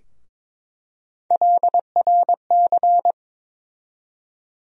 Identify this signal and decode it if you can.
Morse code, heard as a keyed tone, about 22 wpm, 700 Hz. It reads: LRC
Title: USA Call Sign Postfixes - 3 Letters 22wpm